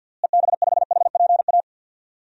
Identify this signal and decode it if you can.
Morse code, heard as a keyed tone, about 53 wpm, 700 Hz. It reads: EB5HPA
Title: DXCC-100 Call Signs - 6 Characters - Repeated 3x - 50wpm